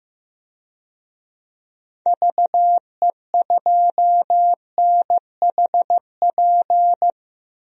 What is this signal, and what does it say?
Morse code, heard as a keyed tone, about 15 wpm, 700 Hz. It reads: VE2NHP
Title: DXCC-100 Call Signs - 6 Characters - 15wpm